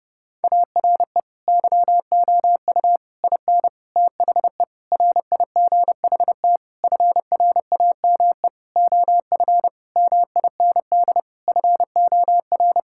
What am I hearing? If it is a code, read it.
Morse code, heard as a keyed tone, about 32 wpm, 700 Hz. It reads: ARE YOU IN THE RIGHT FRAME OF MIND FOR
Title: Sentences from Top 1000 Words - Review Part 6 - 30wpm